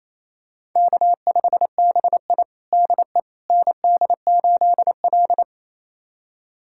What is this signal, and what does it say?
Morse code, heard as a keyed tone, about 28 wpm, 700 Hz. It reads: K5BI DE ND8L
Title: USA Callsign DE Callsign - 4 Characters - 28wpm